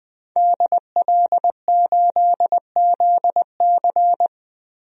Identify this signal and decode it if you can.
Morse code, heard as a keyed tone, about 20 wpm, 700 Hz. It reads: DL8ZC
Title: DXCC-100 Call Signs - 5 Characters - Repeated 3x - 20wpm